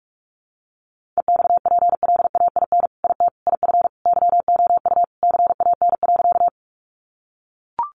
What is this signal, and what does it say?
Morse code, heard as a keyed tone, about 44 wpm, 700 Hz. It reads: EXPLAIN IT IF YOU CAN.
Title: Sentences from Top 1100 Words 45wpm